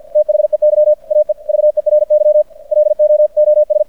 Morse code, heard as a keyed tone, about 50 wpm, 600 Hz. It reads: THEY AE VERY LOOS
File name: archived / CRSnet